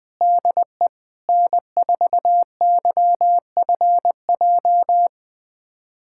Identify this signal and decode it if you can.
Morse code, heard as a keyed tone, about 21 wpm, 700 Hz. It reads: DE N4YFJ
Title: USA Callsign DE Callsign 20wpm